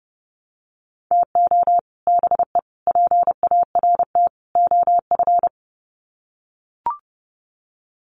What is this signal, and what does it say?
Morse code, heard as a keyed tone, about 30 wpm, 700 Hz. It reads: TO BE PART OF
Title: Sets of 4 Words 30wpm